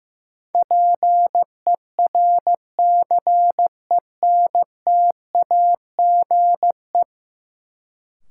Morse code, heard as a keyed tone, about 15 wpm, 700 Hz. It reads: PERCENTAGE
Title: Sentences from Top 1300 Words - New Word Drill Alt - 15wpm